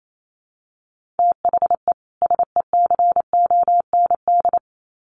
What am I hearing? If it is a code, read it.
Morse code, heard as a keyed tone, about 28 wpm, 700 Hz. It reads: THE SECOND
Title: Sets of 2 Words 28wpm